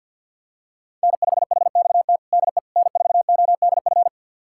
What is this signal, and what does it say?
Morse code, heard as a keyed tone, about 50 wpm, 700 Hz. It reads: N5HXT DE N4ODF
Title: USA Callsign DE Callsign - 5 Characters - 50wpm